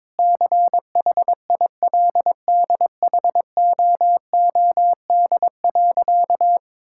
Morse code, heard as a keyed tone, about 21 wpm, 700 Hz. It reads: CHILDHOOD.
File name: practice / 22wpm / Sentences with 2-Word Phrases - 22wpm